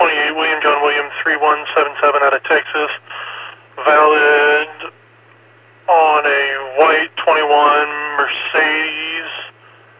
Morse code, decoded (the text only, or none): none